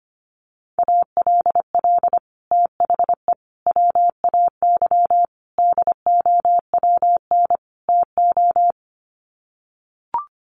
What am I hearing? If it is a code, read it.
Morse code, heard as a keyed tone, about 25 wpm, 700 Hz. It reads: ALL THE WAY DOWN TO
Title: Sets of 5 Words Encore2 25wpm